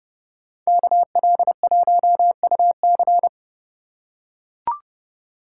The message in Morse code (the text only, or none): KL1UC